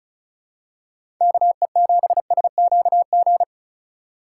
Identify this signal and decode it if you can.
Morse code, heard as a keyed tone, about 34 wpm, 690 Hz. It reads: KE7SQG
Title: USA Call Signs - 3x Character Spacing - 35wpm